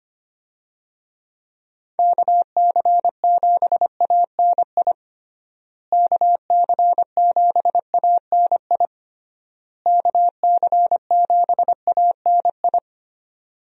Morse code, heard as a keyed tone, about 25 wpm, 700 Hz. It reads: KC7ANI KC7ANI KC7ANI
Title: USA Call Signs - 6 Characters - Repeated 3x - 25wpm